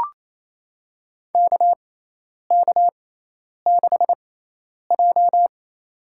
Morse code, heard as a keyed tone, about 28 wpm, spaced as for 10 wpm, 700 Hz. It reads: KK6J
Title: USA Call Signs - 4 Characters - 6x Character Spacing - 28wpm